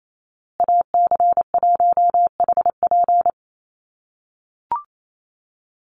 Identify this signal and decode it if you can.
Morse code, heard as a keyed tone, about 28 wpm, 700 Hz. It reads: AC1HP
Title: USA Call Signs Repeated 2x 28wpm